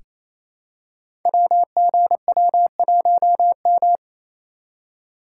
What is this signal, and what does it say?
Morse code, heard as a keyed tone, about 27 wpm, 700 Hz. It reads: WGW1M
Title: Code Groups - 5 Letter:Numbers - 28wpm